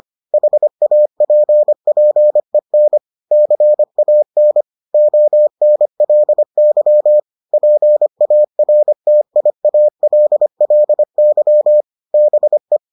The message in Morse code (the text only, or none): HAPPEN CAN ONLY PARTIALLY BE